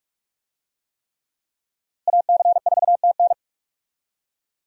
Morse code, heard as a keyed tone, about 47 wpm, 700 Hz. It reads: AC4TN
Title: USA Call Signs - 5 Characters - Repeated 3x - 45wpm